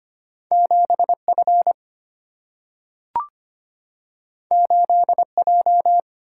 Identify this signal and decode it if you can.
Morse code, heard as a keyed tone, about 25 wpm, 700 Hz. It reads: 7 F 8 J
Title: Single-Letter Single-Number 25wpm